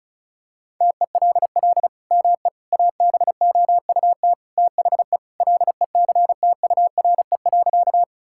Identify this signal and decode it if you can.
Morse code, heard as a keyed tone, about 36 wpm, 700 Hz. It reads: TELL ME ABOUT THE LECTURE.
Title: Sentences from Top 1600 Words 35wpm